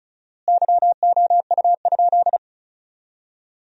Morse code, heard as a keyed tone, about 36 wpm, 700 Hz. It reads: YOU?
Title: 4x Word-Spacing - Sentences from Top 500 Words - 35wpm